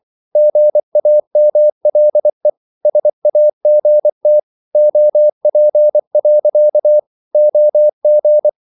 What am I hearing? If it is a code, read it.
Morse code, heard as a keyed tone, about 24 wpm, 600 Hz. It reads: GAMLE SAGT OP. OG
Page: Ferie 24 wpm